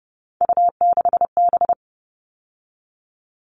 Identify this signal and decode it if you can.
Morse code, heard as a keyed tone, about 30 wpm, 700 Hz. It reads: U6B
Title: All Cut Numbers - 3-Digit 30wpm